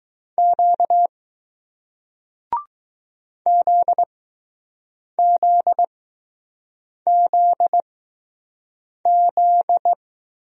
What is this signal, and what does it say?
Morse code, heard as a keyed tone, about 21 wpm, spaced as for 6 wpm, 700 Hz. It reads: Q ZZZZ